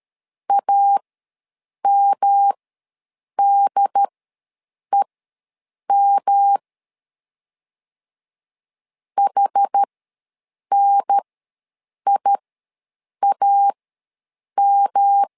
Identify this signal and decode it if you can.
Morse code, heard as a keyed tone, about 13 wpm, spaced as for 7 wpm, 800 Hz. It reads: AMDEM HNIAM
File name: LEZIONI / 3 MWRO / 3 M ws 12wpm 1745